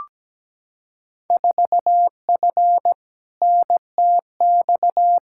Morse code, heard as a keyed tone, about 17 wpm, 700 Hz. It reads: 4F NTX
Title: Field Day Exchanges Encore - 17wpm